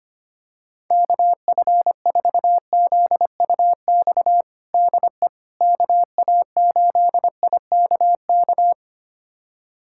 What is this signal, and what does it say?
Morse code, heard as a keyed tone, about 25 wpm, 700 Hz. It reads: KF4ZUX DE KA8IKK
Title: USA Callsign DE Callsign - 6 Characters - 25wpm